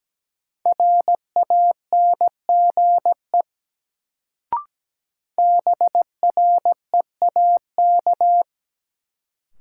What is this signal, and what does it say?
Morse code, heard as a keyed tone, about 17 wpm, 700 Hz. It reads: RANGE BREAK
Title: Sentences from Top 900 Words - New Word Drill - 17wpm